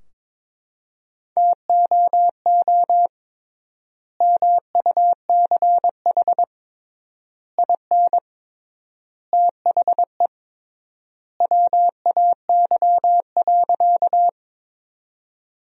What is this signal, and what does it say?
Morse code, heard as a keyed tone, about 23 wpm, 700 Hz. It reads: TOO MUCH IN THE WAY.
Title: 3x Word-Spacing - Sentences from Top 200 Words - 22wpm